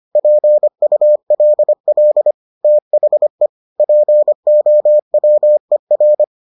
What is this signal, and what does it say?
Morse code, heard as a keyed tone, about 25 wpm, 600 Hz. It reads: PULL THE POWER